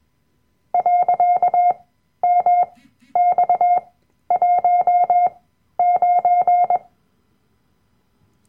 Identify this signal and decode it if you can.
Morse code, heard as a keyed tone, about 21 wpm, spaced as for 12 wpm, 710 Hz. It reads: .MX19
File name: Workshop Dateien / CW-Online-Kurs / Lektion 010